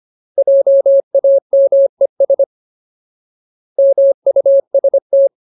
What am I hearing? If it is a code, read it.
Morse code, heard as a keyed tone, about 25 wpm, 550 Hz. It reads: JAMES MUST
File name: Ultimate Word List / 25WPM Ultimate